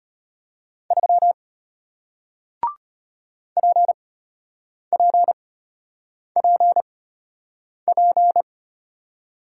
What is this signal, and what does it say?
Morse code, heard as a keyed tone, about 33 wpm, spaced as for 8 wpm, 700 Hz. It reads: 3 PPPP